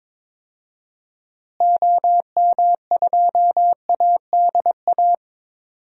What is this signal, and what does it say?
Morse code, heard as a keyed tone, about 22 wpm, 700 Hz. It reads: OM2ADA
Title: DXCC-100 Call Signs - 6 Characters - Repeated 3x - 22wpm